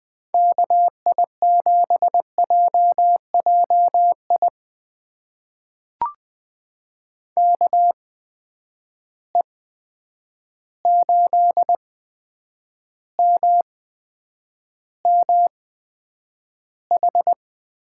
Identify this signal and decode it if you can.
Morse code, heard as a keyed tone, about 20 wpm, spaced as for 9 wpm, 700 Hz. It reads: KI7JJI KE8MMH
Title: USA Call Signs - 6 Characters - 8x Character Spacing - 20wpm